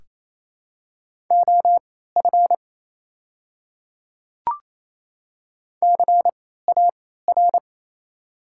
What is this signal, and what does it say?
Morse code, heard as a keyed tone, about 28 wpm, spaced as for 16 wpm, 700 Hz. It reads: OF CAR
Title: 3x Character Spacing - Top 300 Words - 28wpm